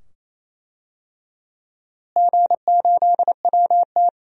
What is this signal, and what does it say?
Morse code, heard as a keyed tone, about 28 wpm, 700 Hz. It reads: G8WT
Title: DXCC-100 Call Signs - 4 Characters - Repeated 2x - 28wpm